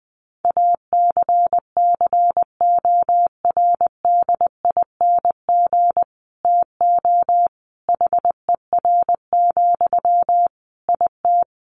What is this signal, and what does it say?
Morse code, heard as a keyed tone, about 19 wpm, 700 Hz. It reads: ACCORDING TO HER, IT
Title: Sentences from Top 800 Words 20wpm